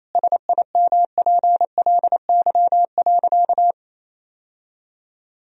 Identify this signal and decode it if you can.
Morse code, heard as a keyed tone, about 28 wpm, 700 Hz. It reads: SIMPLY.
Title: Sentences from Top 1000 Words - Review Part 4 - 28wpm